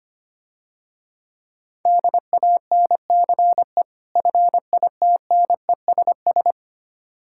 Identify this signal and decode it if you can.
Morse code, heard as a keyed tone, about 25 wpm, 700 Hz. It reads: DANCE FITNESS
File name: practice / 25wpm / Difficult Words - Speed-Racing with Hints - 25wpm